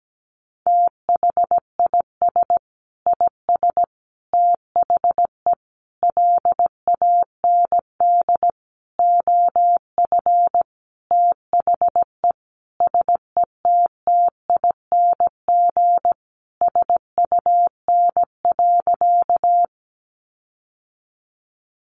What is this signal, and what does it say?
Morse code, heard as a keyed tone, about 17 wpm, 700 Hz. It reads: THIS IS THE LAND OF THE SETTING SUN.
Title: Sentences from Top 1300 Words 17wpm